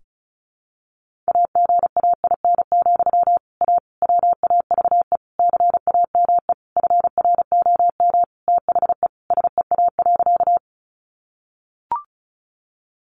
Morse code, heard as a keyed tone, about 34 wpm, 700 Hz. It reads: AGAIN, A WAVE CAME FROM THE SEA.
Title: Sentences from Top 1400 Words 35wpm